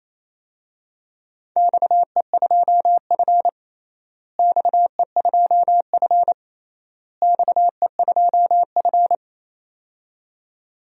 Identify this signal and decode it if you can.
Morse code, heard as a keyed tone, about 27 wpm, 700 Hz. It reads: XE2F XE2F XE2F
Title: DXCC-100 Call Signs - 4 Characters - Repeated 3x - 28wpm